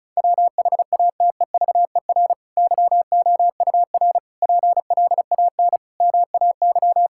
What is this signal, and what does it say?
Morse code, heard as a keyed tone, about 34 wpm, 700 Hz. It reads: WHATEVER YOUR PLAN MAY
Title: Sentences from Top 1600 Words 35wpm